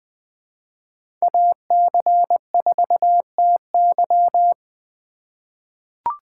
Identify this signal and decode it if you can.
Morse code, heard as a keyed tone, about 20 wpm, 700 Hz. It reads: AC4TY
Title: USA Call Signs - 5 Characters - 2x Character Spacing - 20wpm